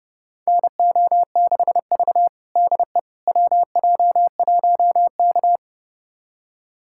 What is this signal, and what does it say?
Morse code, heard as a keyed tone, about 30 wpm, 700 Hz. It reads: NO6V DE WJ1K
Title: USA Callsign DE Callsign - 4 Characters - 30wpm